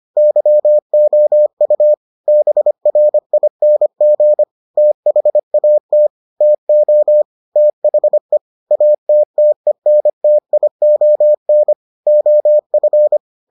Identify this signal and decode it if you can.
Morse code, heard as a keyed tone, about 25 wpm, 600 Hz. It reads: YOU BRING THAT TO THE ATTENTION OF